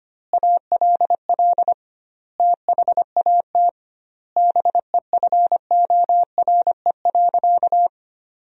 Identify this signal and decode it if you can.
Morse code, heard as a keyed tone, about 25 wpm, 700 Hz. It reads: ALL THAT BEFORE.
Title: 2x Word-Spacing - Sentences from Top 400 Words - 25wpm